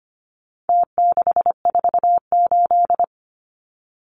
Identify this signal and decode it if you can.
Morse code, heard as a keyed tone, about 25 wpm, 700 Hz. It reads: T648
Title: Numbers - Contest Style - 4-Digit 25wpm